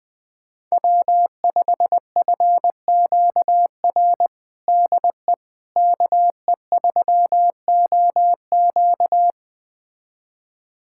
Morse code, heard as a keyed tone, about 20 wpm, 700 Hz. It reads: W5FQR DE KE3OQ
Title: USA Callsign DE Callsign - 5 Characters - 20wpm